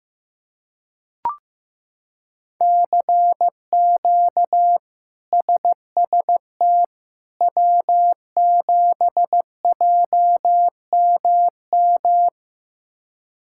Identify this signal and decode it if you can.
Morse code, heard as a keyed tone, about 15 wpm, 700 Hz. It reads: CQ SST W7JMM
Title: K1USN SST QSO Practice - 15wpm